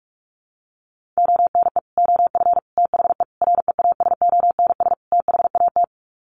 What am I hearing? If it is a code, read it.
Morse code, heard as a keyed tone, about 44 wpm, 700 Hz. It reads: ONE OF THE REASONS THAT
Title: Sets of 5 Words Encore 45wpm